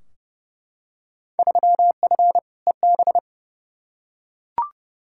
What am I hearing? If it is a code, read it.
Morse code, heard as a keyed tone, about 32 wpm, 700 Hz. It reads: 3F EB
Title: Field Day Exchanges - 30wpm